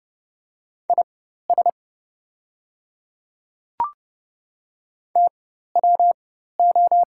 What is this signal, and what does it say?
Morse code, heard as a keyed tone, about 30 wpm, spaced as for 14 wpm, 700 Hz. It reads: IS TWO
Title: 4x Character Spacing - Top 100 Words - 30wpm